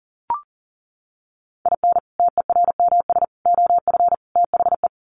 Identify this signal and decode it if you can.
Morse code, heard as a keyed tone, about 37 wpm, 700 Hz. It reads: IN TERMS OF THE
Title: Sets of 4 Words 40wpm